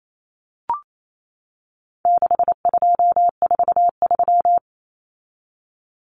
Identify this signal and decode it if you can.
Morse code, heard as a keyed tone, about 28 wpm, 700 Hz. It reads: 6243
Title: Numbers - 4-Digit 28wpm